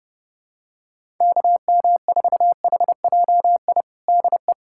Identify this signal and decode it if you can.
Morse code, heard as a keyed tone, about 30 wpm, 700 Hz. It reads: KM4HJI DE
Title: USA Callsign DE Callsign - 6 Characters - 30wpm